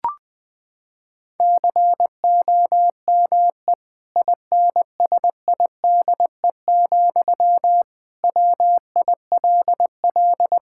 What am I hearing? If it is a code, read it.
Morse code, heard as a keyed tone, about 20 wpm, 700 Hz. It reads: COME INSIDE, WILL
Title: Sentences from Top 1000 Words - Review Part 6 - 20wpm